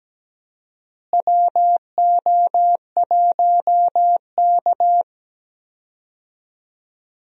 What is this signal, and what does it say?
Morse code, heard as a keyed tone, about 17 wpm, 700 Hz. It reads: WO1K
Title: USA Call Signs - 4 Characters - Repeated 2x - 17wpm